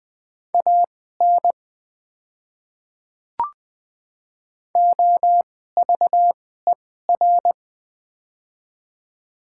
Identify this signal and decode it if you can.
Morse code, heard as a keyed tone, about 20 wpm, spaced as for 15 wpm, 700 Hz. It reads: AN OVER
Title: 2x Character Spacing - Top 100 Words - 20wpm